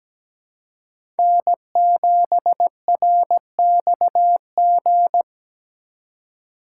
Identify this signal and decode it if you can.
Morse code, heard as a keyed tone, about 17 wpm, 700 Hz. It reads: N7RXG
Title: USA Call Signs - 5 Characters - 2x Character Spacing - 17wpm